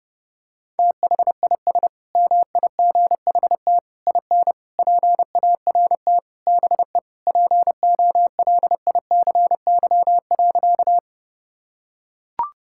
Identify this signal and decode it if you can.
Morse code, heard as a keyed tone, about 30 wpm, 700 Hz. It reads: THIS MIGHT IN PART BE POLICY.
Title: Sentences from Top 1000 Words - Review Part 5 - 30wpm